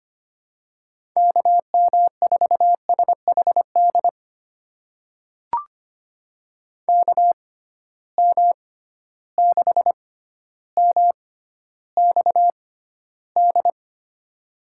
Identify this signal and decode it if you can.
Morse code, heard as a keyed tone, about 25 wpm, spaced as for 13 wpm, 700 Hz. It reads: KM4SHD KM6MXD
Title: USA Call Signs - 6 Characters - 6x Character Spacing - 25wpm